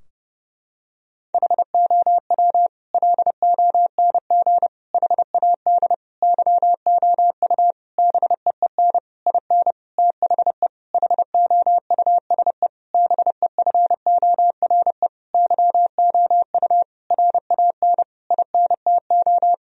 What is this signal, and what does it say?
Morse code, heard as a keyed tone, about 30 wpm, 700 Hz. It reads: HOW LONG HAD YOU BEEN IN THE HOUSE BEFORE YOU RAN INTO
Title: Sentences from Top 1000 Words - Review Part 3 - 30wpm